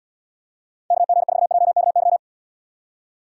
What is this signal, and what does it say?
Morse code, heard as a keyed tone, about 77 wpm, 700 Hz. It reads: KG4JKY